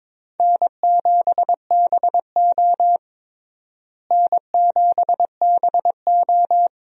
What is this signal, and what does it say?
Morse code, heard as a keyed tone, about 22 wpm, 700 Hz. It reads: N7BO N7BO
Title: USA Call Signs - 4 Characters - Repeated 2x - 22wpm